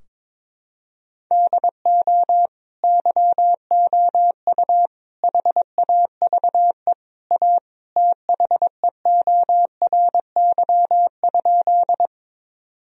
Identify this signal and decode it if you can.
Morse code, heard as a keyed tone, about 22 wpm, 700 Hz. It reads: DO YOU HAVE A THEORY?
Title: Sentences from Top 1000 Words - Review Part 3 - 22wpm